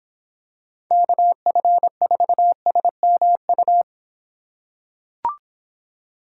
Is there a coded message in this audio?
Yes: Morse code from 0.9 to 3.8 s, heard as a keyed tone, about 26 wpm, 700 Hz. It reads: KF4SMU